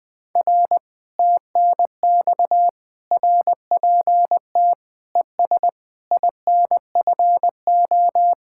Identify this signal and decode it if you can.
Morse code, heard as a keyed tone, about 20 wpm, 700 Hz. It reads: R TNX RPT ES INFO
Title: Speed-Racing Common QSO Phrases Encore - 17wpm